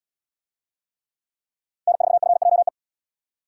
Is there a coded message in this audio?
Yes: Morse code from 1.9 to 2.7 s, heard as a keyed tone, about 78 wpm, 700 Hz. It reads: N4FJE